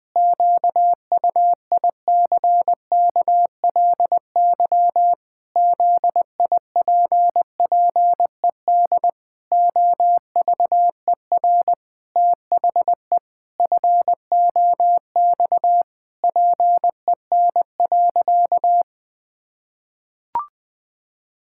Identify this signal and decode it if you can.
Morse code, heard as a keyed tone, about 20 wpm, 700 Hz. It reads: QUICKLY ZIPPED OVER THE FOX PEN.
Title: Pangram Challenge v2 - 20wpm